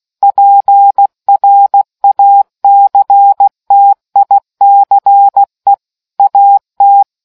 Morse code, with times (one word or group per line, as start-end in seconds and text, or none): 0.2-5.7 PRACTICE
6.2-7.0 AT